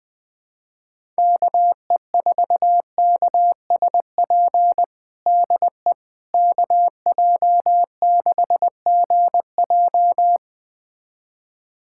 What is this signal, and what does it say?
Morse code, heard as a keyed tone, about 20 wpm, 700 Hz. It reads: KE4KSP DE KJ6GJ
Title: USA Callsign DE Callsign 20wpm